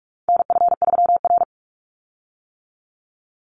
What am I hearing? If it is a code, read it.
Morse code, heard as a keyed tone, about 44 wpm, 700 Hz. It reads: NF3R
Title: Call Signs - 45wpm